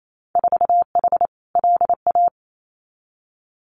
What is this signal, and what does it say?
Morse code, heard as a keyed tone, about 28 wpm, 700 Hz. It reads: 4H LA
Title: Winter Field Day Exchanges Encore 28wpm